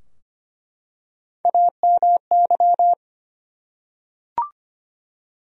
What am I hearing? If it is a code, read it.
Morse code, heard as a keyed tone, about 25 wpm, 700 Hz. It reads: AMY
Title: USA Common Names - Max 4 Letters - 25wpm